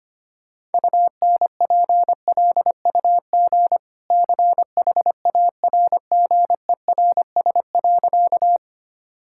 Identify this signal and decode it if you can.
Morse code, heard as a keyed tone, about 25 wpm, 690 Hz. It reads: UNPLUG CHARGERS.
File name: Easy Sentences with Hints - 25wpm